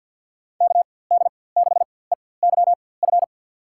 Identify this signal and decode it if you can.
Morse code, heard as a keyed tone, about 51 wpm, spaced as for 23 wpm, 700 Hz. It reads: KD6EYF
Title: USA Call Signs - 6 Characters - 4x Character Spacing - 50wpm